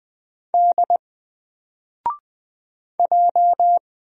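Morse code, heard as a keyed tone, about 20 wpm, 700 Hz. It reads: D J
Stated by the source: Single letter and number Morse drill at 20 wpm, from N4LCD.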